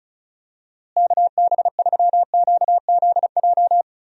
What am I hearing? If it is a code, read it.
Morse code, heard as a keyed tone, about 36 wpm, 700 Hz. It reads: KB3QZJ
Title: USA Call Signs - 6 Characters - Repeated 3x - 35wpm